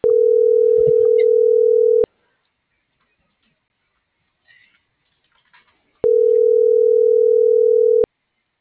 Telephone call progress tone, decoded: ringback tone

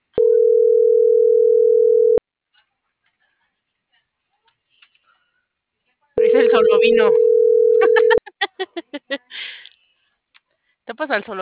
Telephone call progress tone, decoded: ringback tone

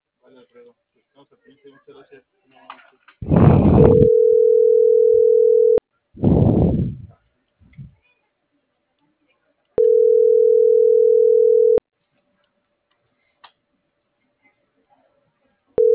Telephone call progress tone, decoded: ringback tone

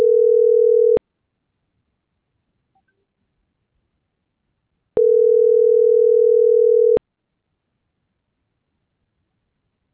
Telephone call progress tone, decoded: ringback tone

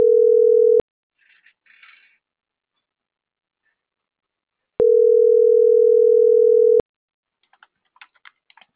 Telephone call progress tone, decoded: ringback tone

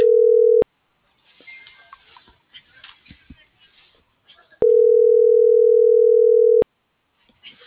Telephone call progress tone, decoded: ringback tone

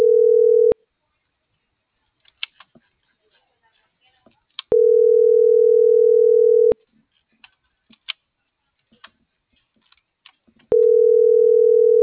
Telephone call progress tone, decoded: ringback tone

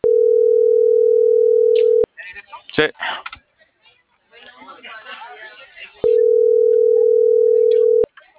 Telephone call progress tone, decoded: ringback tone

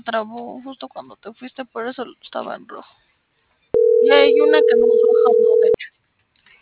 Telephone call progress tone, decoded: ringback tone

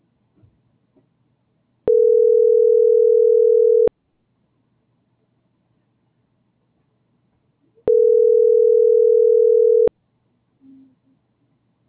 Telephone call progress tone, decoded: ringback tone

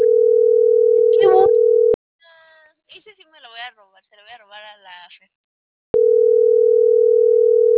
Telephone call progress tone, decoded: ringback tone

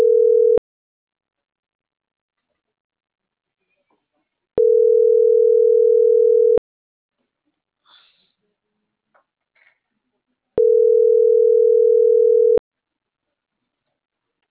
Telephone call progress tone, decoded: ringback tone

